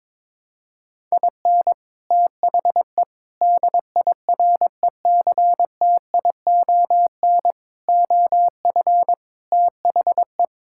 Morse code, heard as a keyed tone, about 22 wpm, 700 Hz. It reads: IN THE DIRECTION OF THE
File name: practice / 22wpm / Build A Phrase - 5-Word Phrases - 22wpm